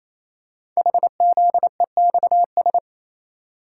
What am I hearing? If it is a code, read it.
Morse code, heard as a keyed tone, about 28 wpm, 700 Hz. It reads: HZEXS
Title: Code Groups - 5 Letters - 28wpm